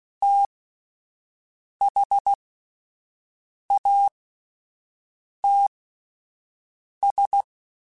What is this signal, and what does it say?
Morse code, heard as a keyed tone, about 16 wpm, spaced as for 5 wpm, 800 Hz. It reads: THATS
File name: QSOs / QSOs-mp3s / WD0FBT 1@10